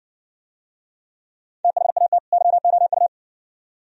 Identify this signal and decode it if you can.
Morse code, heard as a keyed tone, about 57 wpm, 700 Hz. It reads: THAT YOU